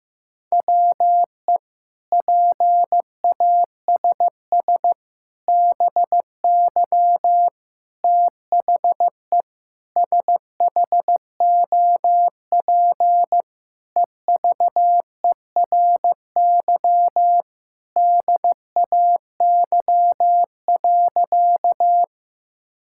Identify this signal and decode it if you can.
Morse code, heard as a keyed tone, about 15 wpm, 700 Hz. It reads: WE PASS BY THE SHOP EVERY DAY.
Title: Sentences from Top 1600 Words 15wpm